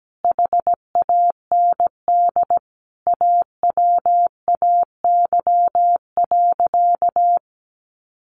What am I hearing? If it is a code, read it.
Morse code, heard as a keyed tone, about 17 wpm, 700 Hz. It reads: HAND AWAY.
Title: A Sound Of Thunder 17wpm